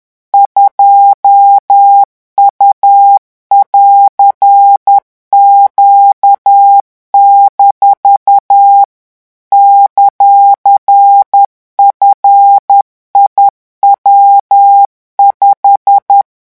Morse code, heard as a keyed tone, about 10 wpm, 800 Hz. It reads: 2U<AR>Q- ;FIW5
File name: morse mp3 / revision19-vitesse 10 mots